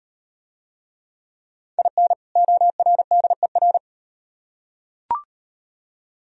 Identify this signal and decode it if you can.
Morse code, heard as a keyed tone, about 37 wpm, 700 Hz. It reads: IN ORDER